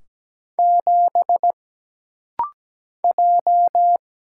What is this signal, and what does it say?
Morse code, heard as a keyed tone, about 17 wpm, 700 Hz. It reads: 7 J
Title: Single Letter-Number - Rapid-Fire 17wpm